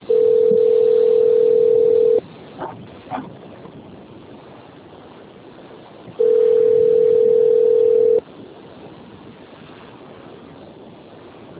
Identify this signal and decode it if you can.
Ringback tone (North America), 440 + 480 Hz.
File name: out- - - - - .25907